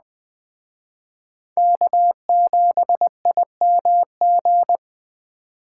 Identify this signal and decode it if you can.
Morse code, heard as a keyed tone, about 20 wpm, 700 Hz. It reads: K7IMG
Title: Speed Racing - USA 1x3 Call Signs 17wpm